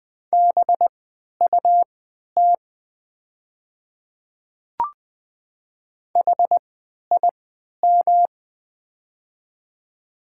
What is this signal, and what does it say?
Morse code, heard as a keyed tone, about 20 wpm, spaced as for 11 wpm, 700 Hz. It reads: BUT HIM
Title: 3x Character Spacing - Top 100 Words - 20wpm